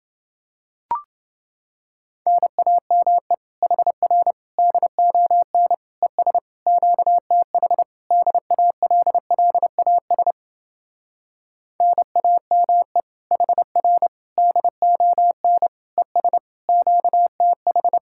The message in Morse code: NAME HR DON ES QTH DALLAS NAME HR DON ES QTH

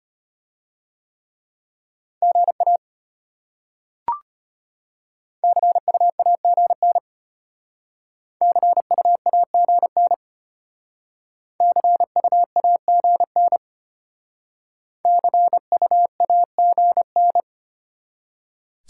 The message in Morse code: GA CU AGN CU AGN CU AGN CU AGN

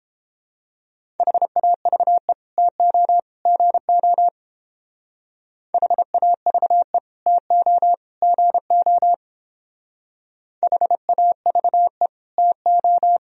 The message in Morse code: HAVE TO GO HAVE TO GO HAVE TO

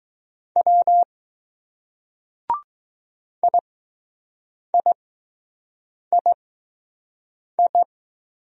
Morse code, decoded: W IIII

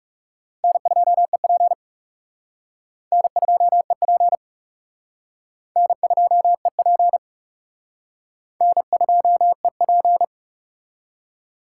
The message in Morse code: N2EP N2EP N2EP N2EP

